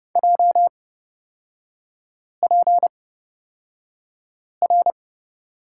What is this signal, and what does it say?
Morse code, heard as a keyed tone, about 31 wpm, spaced as for 5 wpm, 700 Hz. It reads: JPR
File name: Lesson 38e - Letter q - Three Characters 30wpm